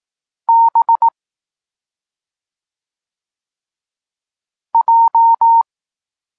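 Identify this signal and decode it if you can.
Morse code, heard as a keyed tone, about 18 wpm, 920 Hz. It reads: B J
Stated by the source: Polish Morse lesson teaching the letter C, from SP9MOA Niepołomicki Klub Krótkofalowców.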